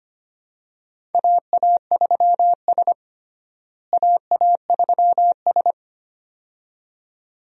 Morse code, heard as a keyed tone, about 25 wpm, 700 Hz. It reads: AA3S AA3S
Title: USA Call Signs - 4 Characters - Repeated 2x - 25wpm